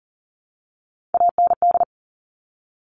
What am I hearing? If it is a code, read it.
Morse code, heard as a keyed tone, about 40 wpm, 700 Hz. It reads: AND
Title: Top Three Letter Combinations in Words 40wpm